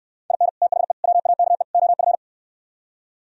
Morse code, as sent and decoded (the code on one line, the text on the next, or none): .. -. / - .... / .-- .- -.- . / --- ..-.
IN THE WAKE OF